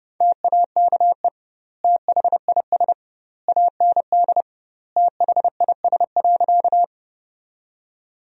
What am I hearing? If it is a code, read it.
Morse code, heard as a keyed tone, about 30 wpm, 700 Hz. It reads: TAKE THIS AND THIS.
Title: 2x Word-Spacing - Sentences from Top 200 Words - 30wpm